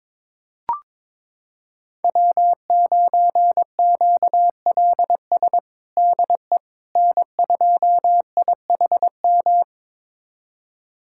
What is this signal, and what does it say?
Morse code, heard as a keyed tone, about 21 wpm, 700 Hz. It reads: W9QLS DE N2IHM
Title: USA Callsign DE Callsign - 5 Characters - 22wpm